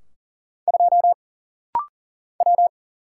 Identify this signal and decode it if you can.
Morse code, heard as a keyed tone, about 39 wpm, 700 Hz. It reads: 2 W
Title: Single Letter-Number Alt - Rapid-Fire 40wpm